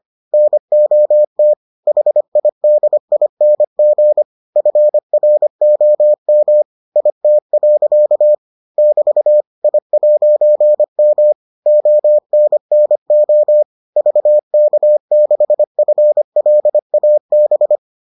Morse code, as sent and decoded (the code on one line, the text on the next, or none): -. --- - / .... .. -.. .. -. --. / ..-. .-. --- -- / .. - .-.-.- / -...- / .. .----. -- / --- -. -. --- / ...- -.- -.... ..-. .-.. .- -...
NOT HIDING FROM IT. <BT> I'M ONNO VK6FLAB